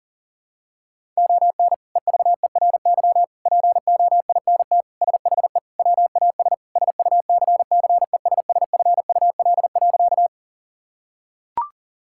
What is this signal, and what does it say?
Morse code, heard as a keyed tone, about 40 wpm, 700 Hz. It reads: ON EVERY POINT SHE WAS SUCCESSFUL.
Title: Sentences from Top 1000 Words - Review Part 6 - 40wpm